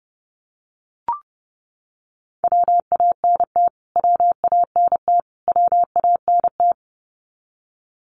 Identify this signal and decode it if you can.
Morse code, heard as a keyed tone, about 30 wpm, 700 Hz. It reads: WANT WANT WANT
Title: Top 500 Words Repeated 30wpm